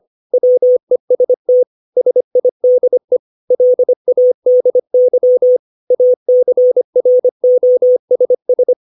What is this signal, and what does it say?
Morse code, heard as a keyed tone, about 25 wpm, 500 Hz. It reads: WEST SIDE LADY ACROSS
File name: 632 words / 632 Words 25WPM